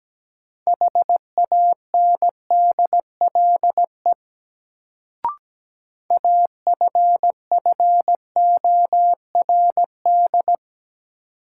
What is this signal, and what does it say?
Morse code, heard as a keyed tone, about 17 wpm, 700 Hz. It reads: HANDLE AFFORD
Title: Sentences from Top 1100 Words - New Word Drill - 17wpm